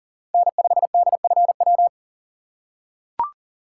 Morse code, heard as a keyed tone, about 40 wpm, 690 Hz. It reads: N5DFW
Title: USA Call Signs - 4x Character Spacing - 40wpm